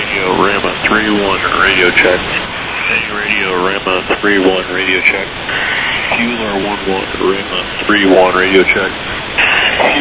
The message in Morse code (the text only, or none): none